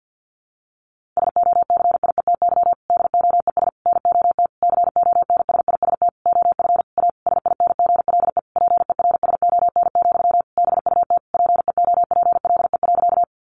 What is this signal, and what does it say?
Morse code, heard as a keyed tone, about 50 wpm, 700 Hz. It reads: SOCIETY DOES NOT CONSIST OF A SINGLE PERSON, BUT PEOPLE.
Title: Sentences from Top 1400 Words 50wpm